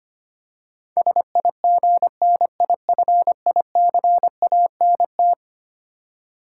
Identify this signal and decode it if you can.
Morse code, heard as a keyed tone, about 25 wpm, 700 Hz. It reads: SIGNIFICANT
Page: morse-code-ninja-practice-25wpm